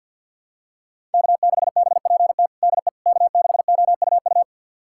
Morse code, heard as a keyed tone, about 46 wpm, 700 Hz. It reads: K6BPT DE K6OUU
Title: USA Callsign DE Callsign - 5 Characters - 50wpm